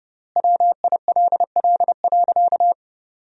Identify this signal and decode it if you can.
Morse code, heard as a keyed tone, about 29 wpm, 700 Hz. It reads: WILL.
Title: 3x Word-Spacing - Sentences from Top 200 Words - 30wpm